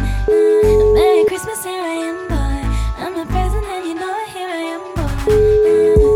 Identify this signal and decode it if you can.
Ringback tone (Europe, 425 Hz).